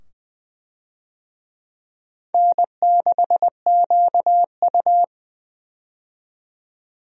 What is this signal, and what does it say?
Morse code, heard as a keyed tone, about 20 wpm, 700 Hz. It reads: N6QU
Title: USA Call Signs - 4 Characters - 20wpm